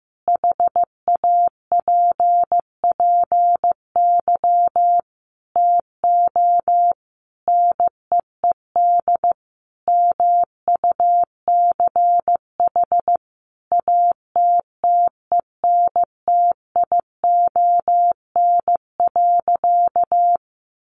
Morse code, heard as a keyed tone, about 15 wpm, 700 Hz. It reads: HAPPY TO NEED MUCH ATTENTION.
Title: Sentences from Top 1000 Words 15wpm